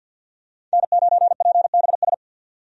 Morse code, heard as a keyed tone, about 46 wpm, 700 Hz. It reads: N9PBS
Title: USA Call Signs - 5 Characters - 50wpm